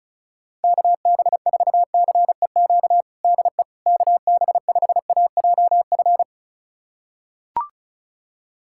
Morse code, heard as a keyed tone, about 34 wpm, 700 Hz. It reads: KB4CEQ DE KB5AJF
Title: USA Callsign DE Callsign 35wpm